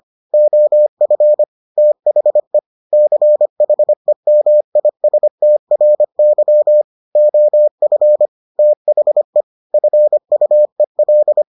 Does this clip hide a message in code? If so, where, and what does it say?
Yes: Morse code from 0.3 to 11.4 s, heard as a keyed tone, about 25 wpm, 600 Hz. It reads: OF THE CHEMISTRY OF THE FUEL